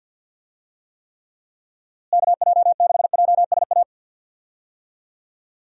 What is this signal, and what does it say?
Morse code, heard as a keyed tone, about 51 wpm, 700 Hz. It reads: KJ6JSA
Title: USA Call Signs - 6 Characters - Repeated 3x - 50wpm